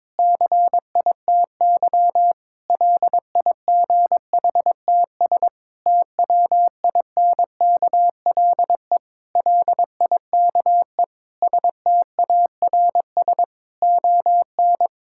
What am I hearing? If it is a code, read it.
Morse code, heard as a keyed tone, about 21 wpm, 700 Hz. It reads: CITY LIGHTS TWINKLE LIKE STARS ON
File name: Difficult Sentences with Hints - 22wpm